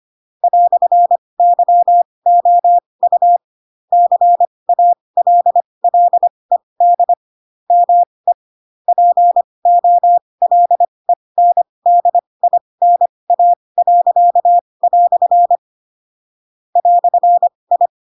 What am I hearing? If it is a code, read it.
Morse code, heard as a keyed tone, about 25 wpm, spaced as for 20 wpm, 700 Hz. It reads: "YOU CALLED ME POLENDINA." "I
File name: Pinocchio / Pinocchio ch2 15 25 ik1hge